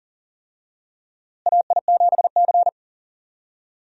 Morse code, heard as a keyed tone, about 40 wpm, 700 Hz. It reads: AI7C
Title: USA Call Signs - 4 Characters - Repeated 2x - 40wpm